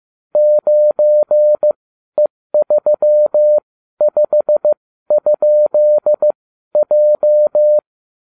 Morse code, heard as a keyed tone, about 15 wpm, spaced as for 12 wpm, 600 Hz. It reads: 9E35?J